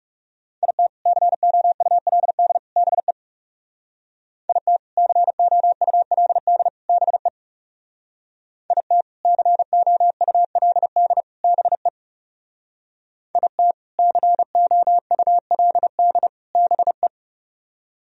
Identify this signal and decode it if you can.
Morse code, heard as a keyed tone, about 35 wpm, 700 Hz. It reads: IT COULD BE IT COULD BE IT COULD BE IT COULD BE
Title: Speed-Racing - Sets of 3 Words 30wpm